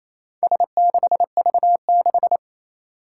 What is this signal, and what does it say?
Morse code, heard as a keyed tone, about 28 wpm, 700 Hz. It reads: S6V6
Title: Four Challenging Letter-Number Combinations 28wpm